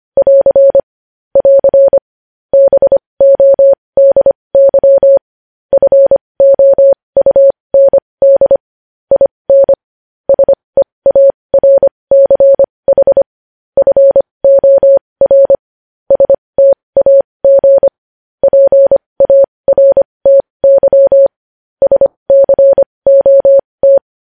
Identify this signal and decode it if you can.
Morse code, heard as a keyed tone, about 25 wpm, spaced as for 20 wpm, 550 Hz. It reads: <AR> <AR> BODY FOUND IN SEARCH FOR STAG PARTY SCOT
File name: podcast / News headlines-1x-25x20